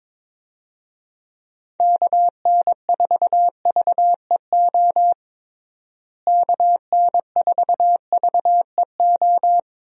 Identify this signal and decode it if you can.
Morse code, heard as a keyed tone, about 22 wpm, 700 Hz. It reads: KN4VEO KN4VEO
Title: USA Call Signs - 6 Characters - Repeated 3x - 22wpm